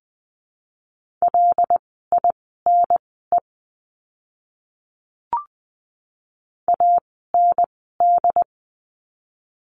Morse code, heard as a keyed tone, about 20 wpm, spaced as for 15 wpm, 700 Hz. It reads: LINE AND
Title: 115 Lesson 19f - Letter w - Words 20wpm